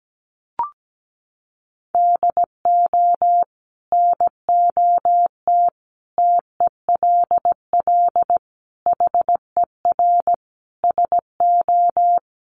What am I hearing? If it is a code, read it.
Morse code, heard as a keyed tone, about 17 wpm, 700 Hz. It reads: DO NOT TELL HER SO
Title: Sentences from Top 1500 Words 17wpm